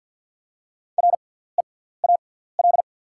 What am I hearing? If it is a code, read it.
Morse code, heard as a keyed tone, about 48 wpm, spaced as for 17 wpm, 700 Hz. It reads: REAL
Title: 6x Character Spacing - Top 300 Words - 50wpm